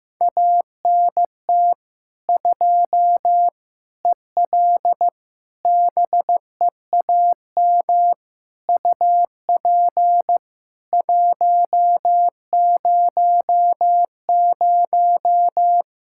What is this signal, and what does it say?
Morse code, heard as a keyed tone, about 15 wpm, 700 Hz. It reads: ANT 2 EL BEAM UP 100
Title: K1USN SST QSO Practice 15wpm